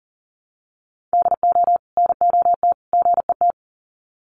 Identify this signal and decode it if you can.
Morse code, heard as a keyed tone, about 40 wpm, 700 Hz. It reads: DO NOT GET
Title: Sets of 3 Words 40wpm